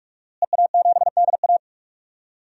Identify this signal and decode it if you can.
Morse code, heard as a keyed tone, about 46 wpm, 700 Hz. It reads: EA7DA
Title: DXCC-100 Call Signs - 5 Characters - Repeated 3x - 45wpm